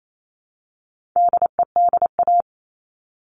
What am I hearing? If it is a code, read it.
Morse code, heard as a keyed tone, about 28 wpm, 700 Hz. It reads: DEDA
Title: All Cut Numbers - 4-Digit 28wpm